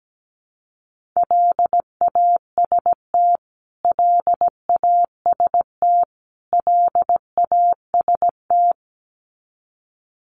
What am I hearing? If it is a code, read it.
Morse code, heard as a keyed tone, about 17 wpm, 700 Hz. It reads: LAST LAST LAST
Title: Top 300 Words Repeated 17wpm